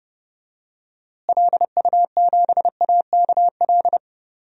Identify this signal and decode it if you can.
Morse code, heard as a keyed tone, about 30 wpm, 700 Hz. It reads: LU7AKL
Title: DXCC-100 Call Signs - 6 Characters - Repeated 3x - 30wpm